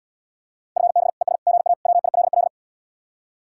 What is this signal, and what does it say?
Morse code, heard as a keyed tone, about 78 wpm, 700 Hz. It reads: FB ES GA WILL